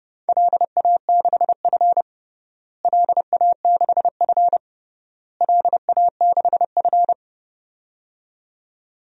Morse code, heard as a keyed tone, about 30 wpm, 700 Hz. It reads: LA6F LA6F LA6F
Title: DXCC-100 Call Signs - 4 Characters - Repeated 3x - 30wpm